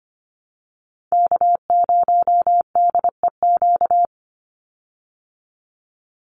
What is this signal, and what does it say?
Morse code, heard as a keyed tone, about 24 wpm, 700 Hz. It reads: K0DEQ
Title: Call Signs Repeated 2x - 25wpm